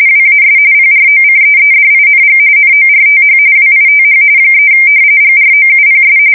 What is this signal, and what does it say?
RTTY, about 45.45 baud. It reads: WAS PROCEEDING NORTH WITH 3 GROUPS
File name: fsk170 / battleshipnj